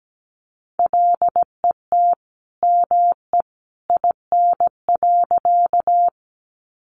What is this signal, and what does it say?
Morse code, heard as a keyed tone, about 17 wpm, 700 Hz. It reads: LET ME IN.
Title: Sentences from Top 300 Words 17wpm